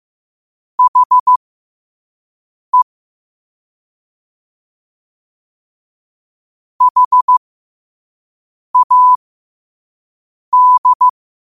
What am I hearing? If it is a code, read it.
Morse code, heard as a keyed tone, about 15 wpm, spaced as for 5 wpm, 1000 Hz. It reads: HE HAD
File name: qotdfiles / 5wpm